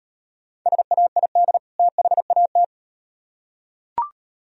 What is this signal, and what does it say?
Morse code, heard as a keyed tone, about 36 wpm, 700 Hz. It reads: SAID THAT